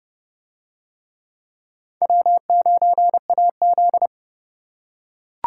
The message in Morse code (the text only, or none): W9AZ